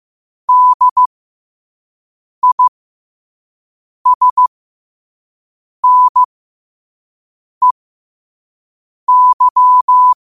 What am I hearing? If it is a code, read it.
Morse code, heard as a keyed tone, about 15 wpm, spaced as for 5 wpm, 1000 Hz. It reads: DISNEY
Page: qotd-at-5-wpm